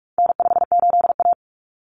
Morse code, heard as a keyed tone, about 44 wpm, 700 Hz. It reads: N58A
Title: Common Cut Numbers - 4-Digit 45wpm